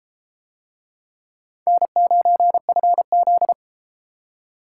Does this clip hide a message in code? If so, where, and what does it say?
Yes: Morse code from 1.7 to 3.5 s, heard as a keyed tone, about 31 wpm, 700 Hz. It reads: N9FZ